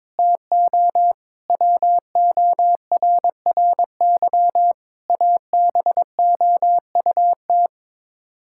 Morse code, heard as a keyed tone, about 22 wpm, 700 Hz. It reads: TO WORRY ABOUT
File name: practice / 22wpm / Sets of 7 Words - 22wpm